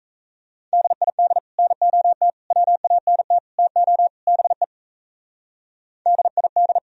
Morse code, heard as a keyed tone, about 33 wpm, 700 Hz. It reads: DID NOT WANT TO BE DID
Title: Speed-Racing - Sets of 5 Words 28wpm